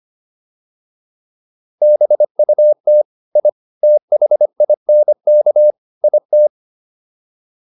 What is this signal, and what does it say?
Morse code, heard as a keyed tone, about 25 wpm, 600 Hz. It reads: BUT I THINK IT